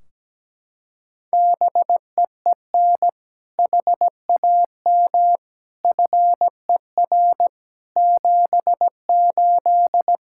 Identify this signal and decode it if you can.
Morse code, heard as a keyed tone, about 17 wpm, 700 Hz. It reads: BEEN HAM FER 78
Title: Practice QSO - Exchange 3 - More Info - 17wpm